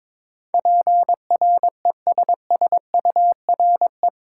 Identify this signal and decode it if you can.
Morse code, heard as a keyed tone, about 21 wpm, 700 Hz. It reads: PRESSURE
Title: Sentences from Top 1000 Words - Review Part 6 - New Word Drill - 22wpm